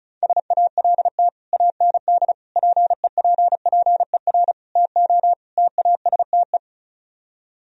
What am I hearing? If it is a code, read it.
Morse code, heard as a keyed tone, about 45 wpm, 700 Hz. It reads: SALT AND PEPPER TO TASTE